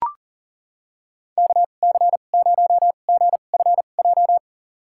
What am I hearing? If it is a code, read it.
Morse code, heard as a keyed tone, about 40 wpm, spaced as for 29 wpm, 700 Hz. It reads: KC0GFJ
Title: USA Call Signs - 6 Characters - 2x Character Spacing - 40wpm